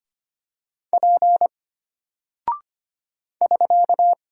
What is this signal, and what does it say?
Morse code, heard as a keyed tone, about 25 wpm, 700 Hz. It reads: P <SK>
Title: Single Character (Letter:Number:Punctuation:Prosign) Alt 25wpm